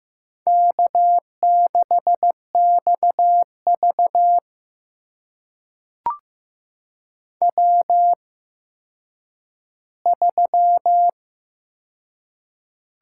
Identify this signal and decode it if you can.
Morse code, heard as a keyed tone, about 15 wpm, spaced as for 9 wpm, 700 Hz. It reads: K6XV W3
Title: USA Call Signs - 4 Characters - 8x Character Spacing - 15wpm